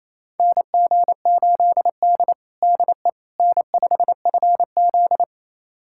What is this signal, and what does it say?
Morse code, heard as a keyed tone, about 27 wpm, 700 Hz. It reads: NG8D DE N5FZ
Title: USA Callsign DE Callsign - 4 Characters - 28wpm